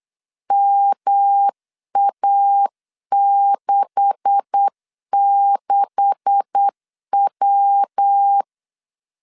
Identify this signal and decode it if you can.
Morse code, heard as a keyed tone, about 8 wpm, 800 Hz. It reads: MA66W
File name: LEZIONI / 9 286 8wpm 3425